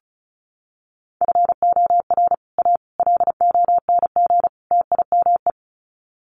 Morse code, heard as a keyed tone, about 34 wpm, 700 Hz. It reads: FOR A LONG TIME
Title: Sets of 4 Words 35wpm